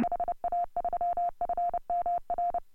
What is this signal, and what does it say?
Morse code, heard as a keyed tone, about 30 wpm, 700 Hz. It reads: HA3FMR
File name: AUDIO2018 / HA3FMR CW